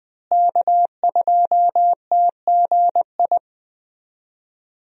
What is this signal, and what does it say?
Morse code, heard as a keyed tone, about 20 wpm, 700 Hz. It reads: K2TGI
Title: USA Call Signs - 5 Characters - Repeated 3x - 20wpm